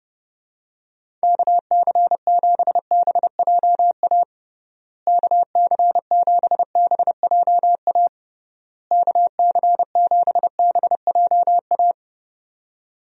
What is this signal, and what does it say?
Morse code, heard as a keyed tone, about 30 wpm, 700 Hz. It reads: KC7BJA KC7BJA KC7BJA
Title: USA Call Signs - 6 Characters - Repeated 3x - 30wpm